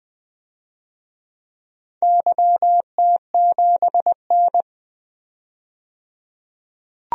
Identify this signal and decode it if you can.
Morse code, heard as a keyed tone, about 20 wpm, 700 Hz. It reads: YT7N
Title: DXCC-100 Call Signs - 4 Characters - 20wpm